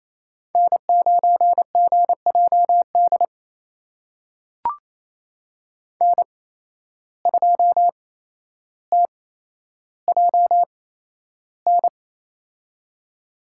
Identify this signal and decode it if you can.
Morse code, heard as a keyed tone, about 28 wpm, spaced as for 12 wpm, 700 Hz. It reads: N9GJD N2TJN
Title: USA Call Signs - 5 Characters - 8x Character Spacing - 28wpm